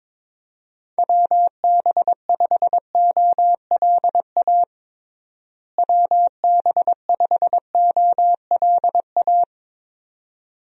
Morse code, heard as a keyed tone, about 22 wpm, 700 Hz. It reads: WB5OLA WB5OLA
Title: USA Call Signs - 6 Characters - Repeated 2x - 22wpm